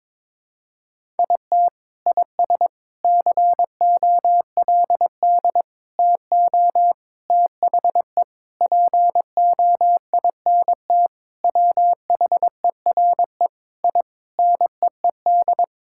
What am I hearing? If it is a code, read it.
Morse code, heard as a keyed tone, about 22 wpm, 700 Hz. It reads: IT IS COLD TO THE POINT WHERE I NEED
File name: practice / 22wpm / Sentences with 4-Word Phrases - 22wpm